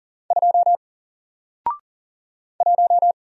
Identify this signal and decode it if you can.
Morse code, heard as a keyed tone, about 40 wpm, 700 Hz. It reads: 2 1
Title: Single Character (Letter:Number:Punctuation:Prosign) Alt 40wpm